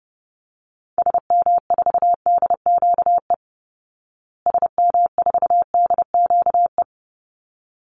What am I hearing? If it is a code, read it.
Morse code, heard as a keyed tone, about 30 wpm, 700 Hz. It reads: SM4DQE SM4DQE
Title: Call Signs Repeated 3x - 30wpm